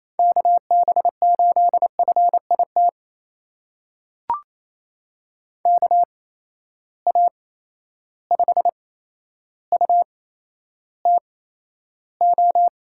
Morse code, heard as a keyed tone, about 28 wpm, spaced as for 12 wpm, 700 Hz. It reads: KB8FIT KA5UTO
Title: USA Call Signs - 8x Character Spacing - 28wpm